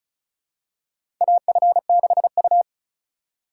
Morse code, heard as a keyed tone, about 34 wpm, 700 Hz. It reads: AF6U
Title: USA Call Signs - 4 Characters - 3x Character Spacing - 35wpm